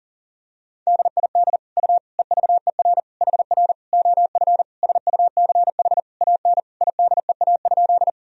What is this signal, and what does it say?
Morse code, heard as a keyed tone, about 37 wpm, 700 Hz. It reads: DID U EVER HR OF SUCH AN IDEA?
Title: Sentences from Top 400 Words and Abbreviations 40wpm